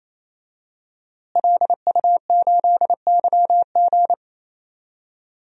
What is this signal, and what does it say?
Morse code, heard as a keyed tone, about 28 wpm, 700 Hz. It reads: LU8YG
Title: DXCC-100 Call Signs - 5 Characters - Repeated 3x - 28wpm